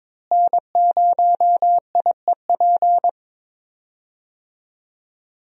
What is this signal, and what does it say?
Morse code, heard as a keyed tone, about 22 wpm, 700 Hz. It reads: N0IEP
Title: USA Call Signs - 5 Characters - Repeated 3x - 22wpm